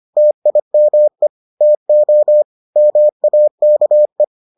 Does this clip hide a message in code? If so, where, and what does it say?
Yes: Morse code from 0.2 to 4.2 s, heard as a keyed tone, about 25 wpm, 600 Hz. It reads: TIME TO MAKE